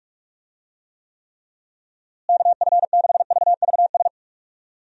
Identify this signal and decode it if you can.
Morse code, heard as a keyed tone, about 46 wpm, 700 Hz. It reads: KF6VVS
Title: USA Call Signs - 6 Characters - Repeated 3x - 45wpm